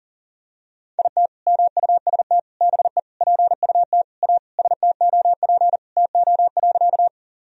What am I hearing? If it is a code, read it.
Morse code, heard as a keyed tone, about 40 wpm, 700 Hz. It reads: IT MUST BE PUT A STOP TO.
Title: Sentences from Top 1000 Words - Review Part 1 - 40wpm